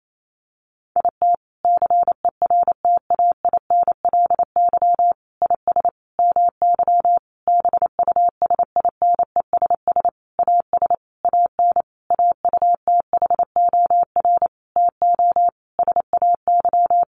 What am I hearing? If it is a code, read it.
Morse code, heard as a keyed tone, about 28 wpm, 700 Hz. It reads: IT CERTAINLY IS MY BUSINESS AS AN AUTHOR TO SAY
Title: Sentences from Top 1200 Words 28wpm